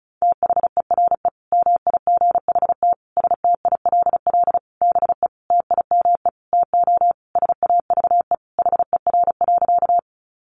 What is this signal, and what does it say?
Morse code, heard as a keyed tone, about 34 wpm, 700 Hz. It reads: THERE MIGHT STILL BE TIME TO SAVE HER.
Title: Sentences from Top 1100 Words 35wpm